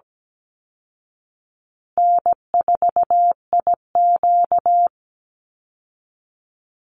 Morse code, heard as a keyed tone, about 17 wpm, 700 Hz. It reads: N4IQ
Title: Easy Call Signs Repeated 3x - 17wpm